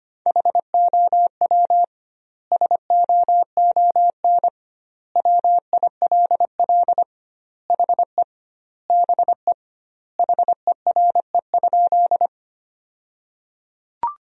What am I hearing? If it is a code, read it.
Morse code, heard as a keyed tone, about 25 wpm, 700 Hz. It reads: HOW SOON WILL HE BE HERE?
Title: 2x Word-Spacing - Sentences from Top 400 Words - 25wpm